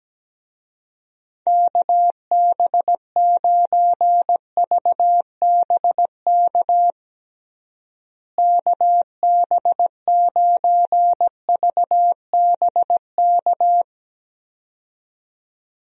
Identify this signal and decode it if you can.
Morse code, heard as a keyed tone, about 17 wpm, 700 Hz. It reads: KB9VBK KB9VBK
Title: USA Call Signs - 6 Characters - Repeated 2x - 17wpm